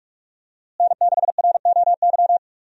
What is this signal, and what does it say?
Morse code, heard as a keyed tone, about 41 wpm, 700 Hz. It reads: N6ROY
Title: USA Call Signs - 5 Characters - Repeated 3x - 45wpm